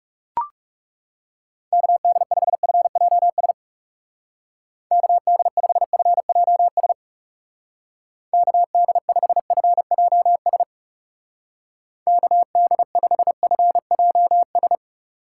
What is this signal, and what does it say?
Morse code, heard as a keyed tone, about 37 wpm, 700 Hz. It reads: KD5FJS KD5FJS KD5FJS KD5FJS
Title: Speed Racing - USA 2x3 Call Signs 30wpm